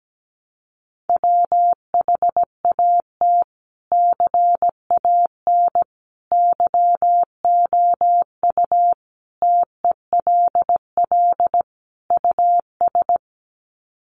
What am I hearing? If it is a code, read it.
Morse code, heard as a keyed tone, about 17 wpm, 700 Hz. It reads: WHAT CAN YOU TELL US
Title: Sets of 5 Words Encore 17wpm